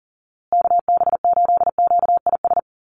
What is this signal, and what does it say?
Morse code, heard as a keyed tone, about 40 wpm, 700 Hz. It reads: KB8QIS
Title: USA Call Signs Repeated 3x 40wpm